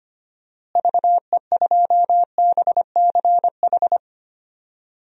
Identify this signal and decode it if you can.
Morse code, heard as a keyed tone, about 25 wpm, 700 Hz. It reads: VE2BCH
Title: DXCC-100 Call Signs - 6 Characters - Repeated 2x - 25wpm